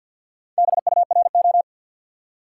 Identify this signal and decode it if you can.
Morse code, heard as a keyed tone, about 67 wpm, spaced as for 58 wpm, 700 Hz. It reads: BURO
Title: Speed-Racing Top 180 QSO Elements 50wpm